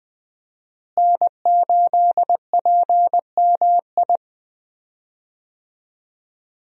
Morse code, heard as a keyed tone, about 20 wpm, 700 Hz. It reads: N8PMI
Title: USA Call Signs - 5 Characters - Repeated 3x - 20wpm